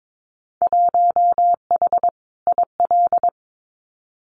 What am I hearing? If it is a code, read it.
Morse code, heard as a keyed tone, about 22 wpm, 700 Hz. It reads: 1H IL
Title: Winter Field Day Exchanges 22wpm